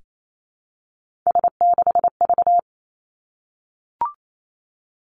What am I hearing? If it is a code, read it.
Morse code, heard as a keyed tone, about 28 wpm, 700 Hz. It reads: S6V
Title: Three Challenging Letter-Number Combinations 28wpm